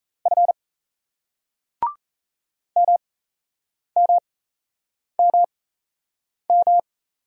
Morse code, heard as a keyed tone, about 38 wpm, spaced as for 8 wpm, 700 Hz. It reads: F MMMM